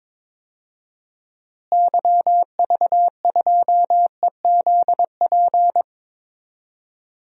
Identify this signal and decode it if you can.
Morse code, heard as a keyed tone, about 22 wpm, 700 Hz. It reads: YV2EZP
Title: DXCC-100 Call Signs - 6 Characters - Repeated 3x - 22wpm